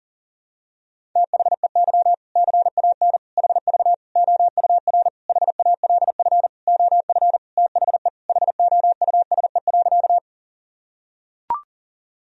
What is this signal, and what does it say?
Morse code, heard as a keyed tone, about 40 wpm, 700 Hz. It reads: THEY CAN HV OUR HALF OF THE HOUSE.
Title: Sentences from Top 400 Words and Abbreviations 40wpm